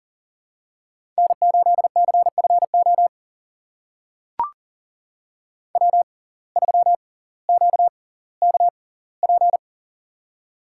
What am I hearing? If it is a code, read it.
Morse code, heard as a keyed tone, about 40 wpm, spaced as for 20 wpm, 700 Hz. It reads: N8CFO W3QKP
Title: USA Call Signs - 5 Characters - 6x Character Spacing - 40wpm